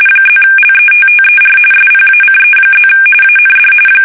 RTTY, about 45.45 baud. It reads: +LETTER SPACE+ AND